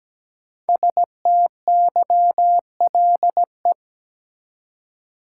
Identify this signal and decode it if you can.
Morse code, heard as a keyed tone, about 17 wpm, 700 Hz. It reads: STYLE
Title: Five Letter Words - 17wpm